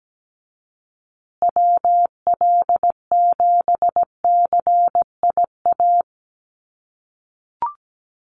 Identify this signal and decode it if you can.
Morse code, heard as a keyed tone, about 17 wpm, 700 Hz. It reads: WL7CIA
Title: USA Call Signs Repeated 2x 17wpm